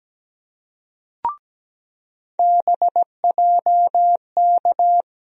Morse code, heard as a keyed tone, about 17 wpm, 700 Hz. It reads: BJK
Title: USA Call Sign Postfixes - 3 Letters 17wpm